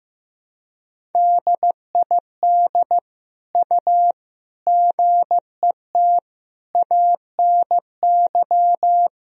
Morse code, heard as a keyed tone, about 15 wpm, 700 Hz. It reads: DID U GET ANY
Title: Sentences from Top 400 Words and Abbreviations 15wpm